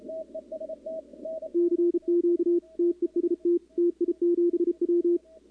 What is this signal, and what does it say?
Morse code, heard as a keyed tone, about 32 wpm, 340 Hz, with radio noise. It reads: CQ TEST TI7W
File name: contests / 2019-cw-wpx-kt5j-k5pi- -2330 0-11